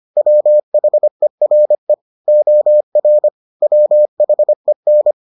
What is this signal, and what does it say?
Morse code, heard as a keyed tone, about 25 wpm, 600 Hz. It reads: WHERE OR WHEN